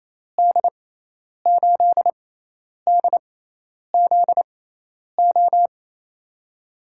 Morse code, heard as a keyed tone, about 28 wpm, spaced as for 10 wpm, 700 Hz. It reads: D8DZO
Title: USA Call Signs - 6 Characters - 6x Character Spacing - 28wpm